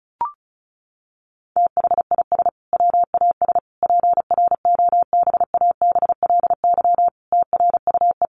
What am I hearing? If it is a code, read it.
Morse code, heard as a keyed tone, about 34 wpm, 700 Hz. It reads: THIS WAS PROBABLY TRUE
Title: Sentences from Top 700 Words 35wpm